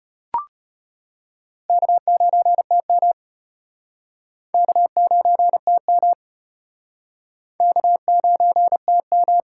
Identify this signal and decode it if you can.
Morse code, heard as a keyed tone, about 32 wpm, 700 Hz. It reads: K9TM K9TM K9TM